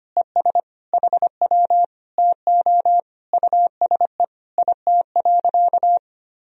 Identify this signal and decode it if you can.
Morse code, heard as a keyed tone, about 25 wpm, 700 Hz. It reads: ES HW TO USE IT.
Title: Sentences from Top 500 Words and Abbreviations 25wpm